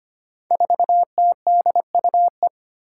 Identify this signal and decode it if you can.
Morse code, heard as a keyed tone, about 25 wpm, 700 Hz. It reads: 4TDUE
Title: Five Challenging Letter-Number Combinations 25wpm